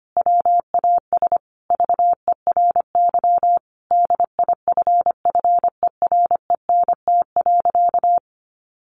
Morse code, heard as a keyed tone, about 25 wpm, 700 Hz. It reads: WAS VERY DIFFERENT.
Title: Sentences from Top 1300 Words 25wpm